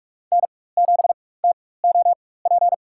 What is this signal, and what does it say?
Morse code, heard as a keyed tone, about 43 wpm, spaced as for 21 wpm, 700 Hz. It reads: N7TOP
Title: USA Call Signs - 5 Characters - 4x Character Spacing - 45wpm